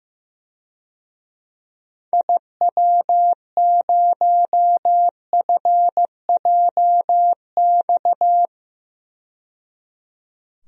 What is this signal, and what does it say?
Morse code, heard as a keyed tone, about 15 wpm, 700 Hz. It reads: IW0FJX
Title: DXCC-100 Call Signs - 6 Characters - 15wpm